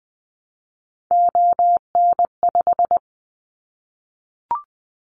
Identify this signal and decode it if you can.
Morse code, heard as a keyed tone, about 20 wpm, 700 Hz. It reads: ON5
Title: Callsign Prefixes for Top 100 DXCC - 20wpm